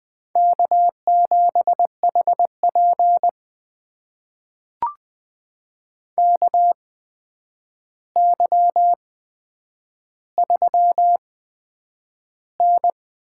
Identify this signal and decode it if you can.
Morse code, heard as a keyed tone, about 20 wpm, spaced as for 9 wpm, 700 Hz. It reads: K7HP KY3N
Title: USA Call Signs - 4 Characters - 8x Character Spacing - 20wpm